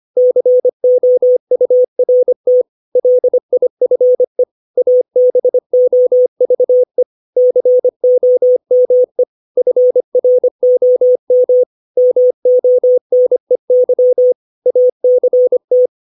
Morse code, heard as a keyed tone, about 25 wpm, 500 Hz. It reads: COURT LIFE ABOVE COME FROM MONEY ACT
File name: 632 words / 632 Words 25WPM